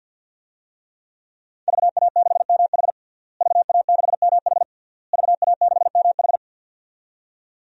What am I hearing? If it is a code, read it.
Morse code, heard as a keyed tone, about 49 wpm, 700 Hz. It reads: VA6MH VA6MH VA6MH
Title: DXCC-100 Call Signs - 5 Characters - Repeated 3x - 50wpm